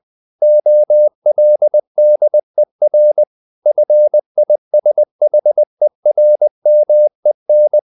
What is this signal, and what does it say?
Morse code, heard as a keyed tone, about 20 wpm, 600 Hz. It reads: OLDER FISHERMEN
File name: the old man and the sea part 2 20wpm